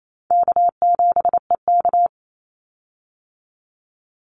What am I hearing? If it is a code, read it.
Morse code, heard as a keyed tone, about 28 wpm, 700 Hz. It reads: K7EK
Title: Easy Call Signs Repeated 2x - 28wpm